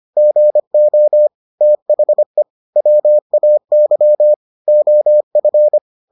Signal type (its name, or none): Morse code